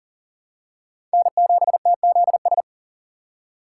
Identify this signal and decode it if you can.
Morse code, heard as a keyed tone, about 40 wpm, 700 Hz. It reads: N7TZS
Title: USA Call Signs - 5 Characters - 4x Character Spacing - 40wpm